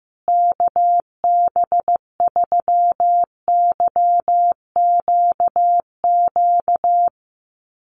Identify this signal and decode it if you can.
Morse code, heard as a keyed tone, about 15 wpm, 700 Hz. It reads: KB3YQQ
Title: USA Call Signs Repeated 3x 15wpm